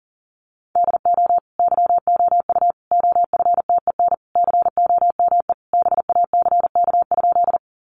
Morse code, heard as a keyed tone, about 40 wpm, 700 Hz. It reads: DO YOU OFTEN COME BACK?
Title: Sentences from Top 300 Words 40wpm